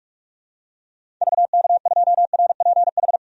Morse code, heard as a keyed tone, about 41 wpm, 700 Hz. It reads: VK2RPH
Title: DXCC-100 Call Signs - 6 Characters - 45wpm